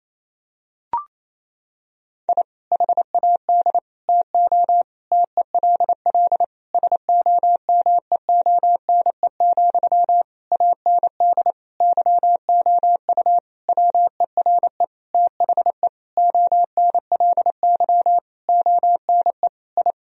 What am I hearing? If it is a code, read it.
Morse code, heard as a keyed tone, about 28 wpm, 700 Hz. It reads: I HAD TO TELL SOMEONE, AND YOU WERE THE ONLY ONE I
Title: Sentences from Top 1000 Words - Review Part 4 - 28wpm